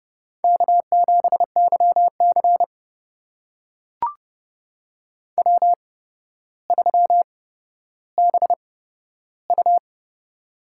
Morse code, heard as a keyed tone, about 30 wpm, spaced as for 13 wpm, 700 Hz. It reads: K7YC W3BU
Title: USA Call Signs - 4 Characters - 8x Character Spacing - 30wpm